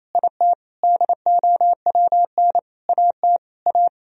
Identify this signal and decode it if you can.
Morse code, heard as a keyed tone, about 28 wpm, 700 Hz. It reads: IT DOWN AT A
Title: Sentences from Top 1000 Words - Review Part 3 - 28wpm